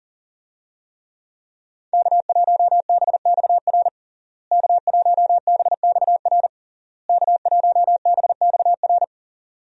Morse code, heard as a keyed tone, about 40 wpm, 700 Hz. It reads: K1BXR K1BXR K1BXR
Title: USA Call Signs - 5 Characters - Repeated 3x - 40wpm